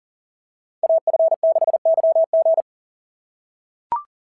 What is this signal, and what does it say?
Morse code, heard as a keyed tone, about 41 wpm, 690 Hz. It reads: AF6YG
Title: SOTA QSO Practice 40wpm